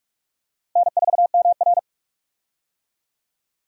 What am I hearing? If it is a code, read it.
Morse code, heard as a keyed tone, about 46 wpm, 700 Hz. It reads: N4MR
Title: USA Call Signs - 4 Characters - Repeated 3x - 45wpm